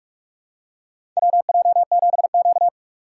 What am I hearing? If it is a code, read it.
Morse code, heard as a keyed tone, about 45 wpm, 700 Hz. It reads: WJ7Q
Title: USA Call Signs - 4 Characters - 6x Character Spacing - 45wpm